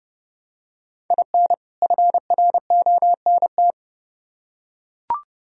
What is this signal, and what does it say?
Morse code, heard as a keyed tone, about 28 wpm, 700 Hz. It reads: IN FRONT